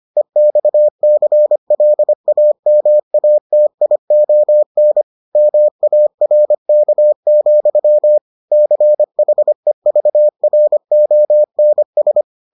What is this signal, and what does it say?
Morse code, heard as a keyed tone, about 25 wpm, 600 Hz. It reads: EXCLAMATION MARK, CHEVRONS